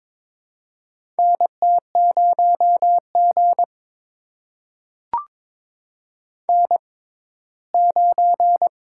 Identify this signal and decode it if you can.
Morse code, heard as a keyed tone, about 22 wpm, spaced as for 15 wpm, 700 Hz. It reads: NT0G N9
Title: USA Call Signs - 6x Character Spacing - 22wpm